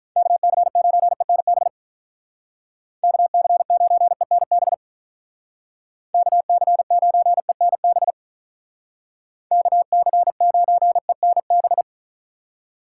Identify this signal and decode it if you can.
Morse code, heard as a keyed tone, about 46 wpm, 700 Hz. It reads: KC9ENB KC9ENB KC9ENB KC9ENB